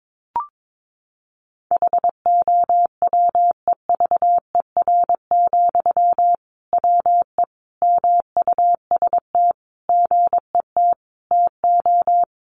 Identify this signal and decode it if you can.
Morse code, heard as a keyed tone, about 22 wpm, 700 Hz. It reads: HOWEVER, WE MUST GET TO
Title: Sentences from Top 600 Words 22wpm